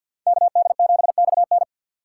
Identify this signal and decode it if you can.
Morse code, heard as a keyed tone, about 62 wpm, 700 Hz. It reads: KD7XN
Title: Speed Racing - USA 2x2 Call Signs 50wpm